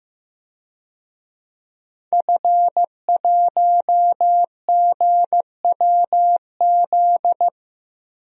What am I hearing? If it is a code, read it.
Morse code, heard as a keyed tone, about 15 wpm, 700 Hz. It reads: F1GWZ
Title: DXCC-100 Call Signs - 5 Characters - Repeated 2x - 15wpm